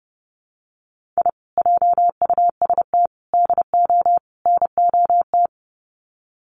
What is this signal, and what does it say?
Morse code, heard as a keyed tone, about 30 wpm, 700 Hz. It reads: I JUST DO NOT
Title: Sets of 4 Words 30wpm